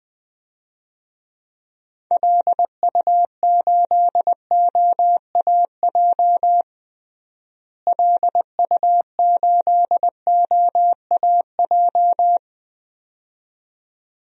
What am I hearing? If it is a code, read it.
Morse code, heard as a keyed tone, about 20 wpm, 700 Hz. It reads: LU8OAJ LU8OAJ
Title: DXCC-100 Call Signs - 6 Characters - Repeated 2x - 20wpm